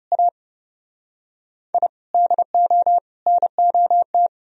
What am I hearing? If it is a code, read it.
Morse code, heard as a keyed tone, about 37 wpm, 700 Hz. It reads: A I DO NOT
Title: Speed-Racing - Sets of 5 Words 30wpm